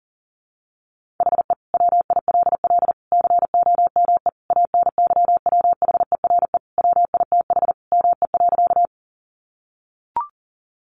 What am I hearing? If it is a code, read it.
Morse code, heard as a keyed tone, about 40 wpm, 700 Hz. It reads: HE WILL COME ANYWHERE WITH ME.
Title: Sentences from Top 1400 Words 40wpm